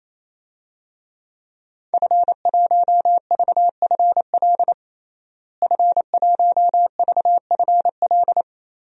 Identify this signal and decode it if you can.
Morse code, heard as a keyed tone, about 28 wpm, 700 Hz. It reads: F1VFL F1VFL
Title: DXCC-100 Call Signs - 5 Characters - Repeated 3x - 28wpm